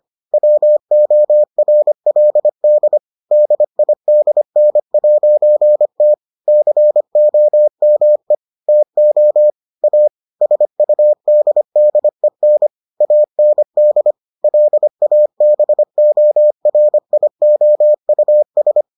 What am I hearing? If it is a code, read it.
Morse code, heard as a keyed tone, about 25 wpm, 600 Hz. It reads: WORLD DIDN'T COME TO A SUDDEN AND LABORIOUS